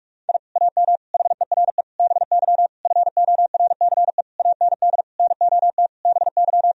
Morse code, heard as a keyed tone, about 43 wpm, 700 Hz. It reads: I AM HERE BY FORCE AND NOT BY
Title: Sentences from Top 1000 Words - Review Part 3 - 45wpm